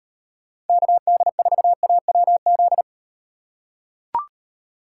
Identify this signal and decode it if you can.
Morse code, heard as a keyed tone, about 38 wpm, 700 Hz. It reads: KD4AWZ